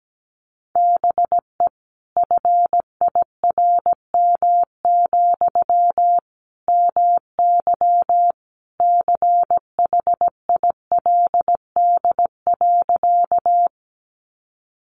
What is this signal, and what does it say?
Morse code, heard as a keyed tone, about 17 wpm, 700 Hz. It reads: BE FIRM, MY CHILD.
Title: Sentences from Top 1400 Words 17wpm